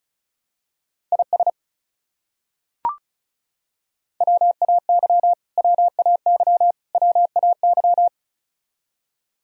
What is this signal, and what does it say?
Morse code, heard as a keyed tone, about 34 wpm, 700 Hz. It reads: IS WAY WAY WAY
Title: Top 100 Words Repeated 35wpm